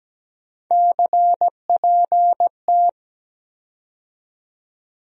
Morse code, heard as a keyed tone, about 17 wpm, 700 Hz. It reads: CPT
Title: USA Call Sign Postfixes - 3 Letters 17wpm